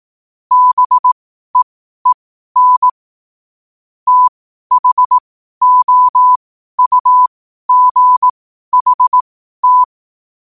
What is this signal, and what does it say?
Morse code, heard as a keyed tone, about 18 wpm, spaced as for 12 wpm, 1000 Hz. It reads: BEEN THOUGHT
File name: qotdfiles / 12wpm